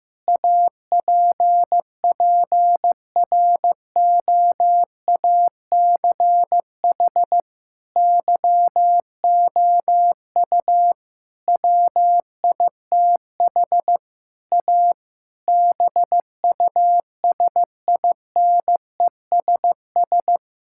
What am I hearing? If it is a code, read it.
Morse code, heard as a keyed tone, about 15 wpm, 700 Hz. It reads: APPROACH YOU WITH A BUSINESS
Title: Sentences from Top 1000 Words - Review Part 6 - 15wpm